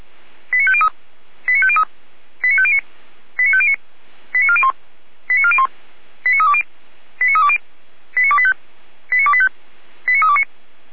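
Five-tone selective call (ZVEI), heard as quick runs of five tones, tones of about 70 ms. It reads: 89503 89503 89509 89509 89401 89401 89320 89320 89285 89285 89329
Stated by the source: ZVEI five-tone call received by SDR, from audio.f85360.feuerwehr-orlamuende.org fire brigade server.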